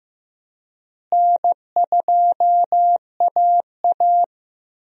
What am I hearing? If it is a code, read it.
Morse code, heard as a keyed tone, about 15 wpm, 700 Hz. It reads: N2AA
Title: USA Call Signs - 4 Characters - Repeated 3x - 15wpm